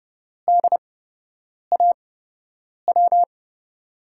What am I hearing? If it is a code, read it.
Morse code, heard as a keyed tone, about 32 wpm, spaced as for 8 wpm, 700 Hz. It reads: DAW
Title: USA Call Signs - 5 Characters - 8x Character Spacing - 30wpm